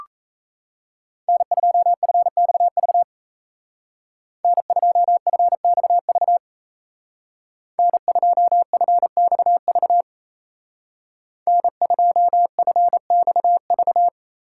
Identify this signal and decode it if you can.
Morse code, heard as a keyed tone, about 36 wpm, 700 Hz. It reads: N2FXV N2FXV N2FXV N2FXV